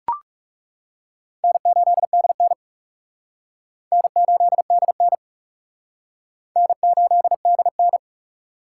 Morse code, heard as a keyed tone, about 38 wpm, 700 Hz. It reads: N8DN N8DN N8DN